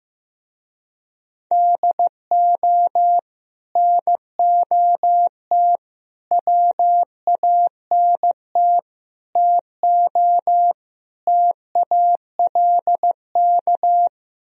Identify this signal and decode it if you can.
Morse code, heard as a keyed tone, about 15 wpm, 700 Hz. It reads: DO NOT WANT TO TALK
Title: Speed-Racing - Sets of 5 Words 15wpm